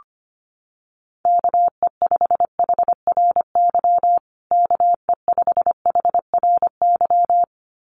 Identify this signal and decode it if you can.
Morse code, heard as a keyed tone, about 25 wpm, 700 Hz. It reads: KE5HRY KE5HRY
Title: USA Call Signs Repeated 2x 25wpm